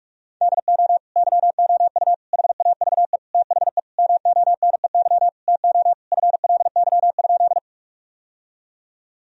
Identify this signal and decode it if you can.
Morse code, heard as a keyed tone, about 46 wpm, 700 Hz. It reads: DO YOU HAVE THE MONEY TO FLY?
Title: Sentences from Top 1000 Words - Review Part 2 - 45wpm